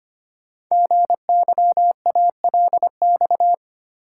Morse code, heard as a keyed tone, about 25 wpm, 700 Hz. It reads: GYALX
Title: Code Groups - 5 Letters - 25wpm